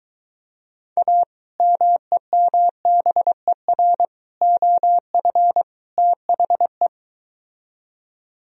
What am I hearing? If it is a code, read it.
Morse code, heard as a keyed tone, about 23 wpm, 700 Hz. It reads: A MEMBER OF THE